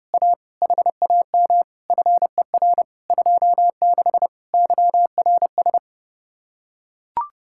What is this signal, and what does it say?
Morse code, heard as a keyed tone, about 30 wpm, 700 Hz. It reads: A HAM FER 26 YRS